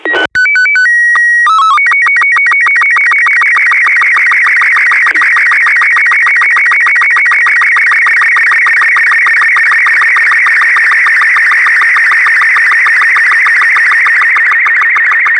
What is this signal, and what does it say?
SSTV (slow-scan television), Robot 36.
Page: prijem-sstv-obrazku-z-iss